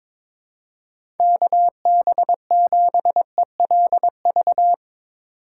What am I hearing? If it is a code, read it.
Morse code, heard as a keyed tone, about 22 wpm, 700 Hz. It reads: KB7ELV
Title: USA Call Signs - 6 Characters - 2x Character Spacing - 22wpm